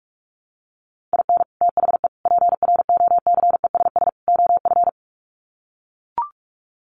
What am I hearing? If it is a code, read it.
Morse code, heard as a keyed tone, about 44 wpm, 700 Hz. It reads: IN THE PROCESS OF
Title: Sets of 4 Words 45wpm